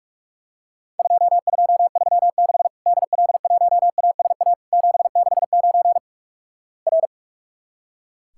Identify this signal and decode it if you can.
Morse code, heard as a keyed tone, about 46 wpm, 700 Hz. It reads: 2236 DL1ASA 769 R
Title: Worked All Europe DX Contest - QTC Exchange - 45wpm